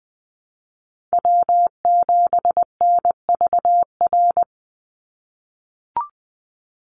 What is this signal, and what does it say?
Morse code, heard as a keyed tone, about 20 wpm, 700 Hz. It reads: W7NVR
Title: USA Call Signs 20wpm